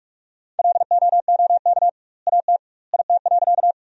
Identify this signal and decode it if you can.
Morse code, heard as a keyed tone, about 46 wpm, 700 Hz. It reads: LOOK AT IT.
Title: 2x Word-Spacing - Sentences from Top 100 Words - 45wpm